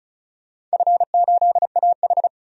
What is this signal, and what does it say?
Morse code, heard as a keyed tone, about 35 wpm, 700 Hz. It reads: F8AH
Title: DXCC-100 Call Signs - 4 Characters - 35wpm